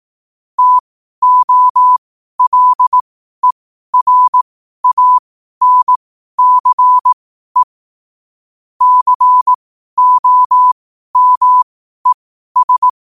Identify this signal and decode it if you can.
Morse code, heard as a keyed tone, about 18 wpm, spaced as for 12 wpm, 1000 Hz. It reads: TOLERANCE COMES
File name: 12wpm